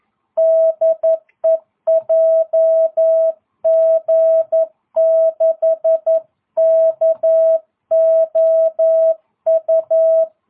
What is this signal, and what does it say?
Morse code, heard as a keyed tone, about 11 wpm, 650 Hz. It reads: DE JG6KOU